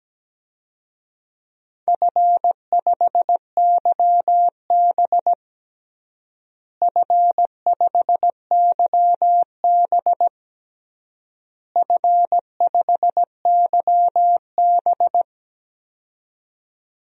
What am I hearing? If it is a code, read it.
Morse code, heard as a keyed tone, about 17 wpm, 700 Hz. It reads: F5YB F5YB F5YB
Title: DXCC-100 Call Signs - 4 Characters - Repeated 3x - 17wpm